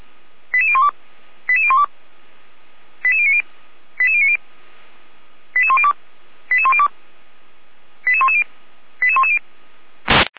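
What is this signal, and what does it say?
Five-tone selective call (ZVEI), heard as quick runs of five tones, tones of about 80 ms. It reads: 88012 88012 88090 88090 88102 88102 88110 88110
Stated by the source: ZVEI five-tone selective call 88199, received by SDR, from a server of Feuerwehr Orlamünde (fire brigade).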